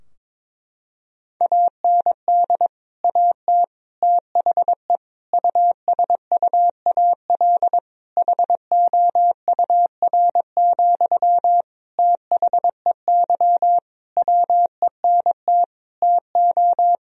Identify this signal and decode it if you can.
Morse code, heard as a keyed tone, about 22 wpm, 700 Hz. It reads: AND AT THE USUAL HOUR, THEY WENT TO
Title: Sentences from Top 1000 Words - Review Part 2 - 22wpm